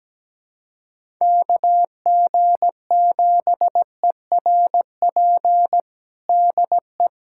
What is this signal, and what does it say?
Morse code, heard as a keyed tone, about 17 wpm, 700 Hz. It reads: KG7ERP DE
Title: USA Callsign DE Callsign 17wpm